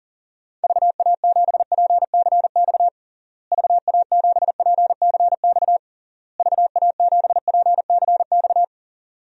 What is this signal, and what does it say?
Morse code, heard as a keyed tone, about 37 wpm, 700 Hz. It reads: VA7PCX VA7PCX VA7PCX
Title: DXCC-100 Call Signs - 6 Characters - Repeated 3x - 40wpm